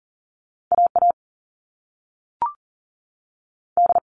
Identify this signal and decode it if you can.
Morse code, heard as a keyed tone, about 42 wpm, 700 Hz. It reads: A A D
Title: All Cut Numbers - 2-Digit 40wpm